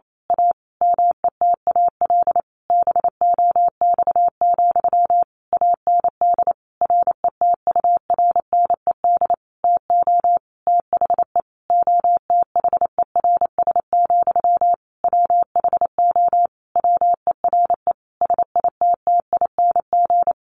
Morse code, heard as a keyed tone, about 28 wpm, 700 Hz. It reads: A METAL BOX, AND RETURNED TO THE OTHERS, WHO WERE SITTING
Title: A Sound Of Thunder 28wpm